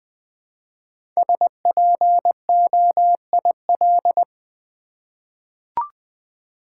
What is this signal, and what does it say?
Morse code, heard as a keyed tone, about 20 wpm, 700 Hz. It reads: SPOIL
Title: 5 Letter Words - First 3 Letters Given - 20wpm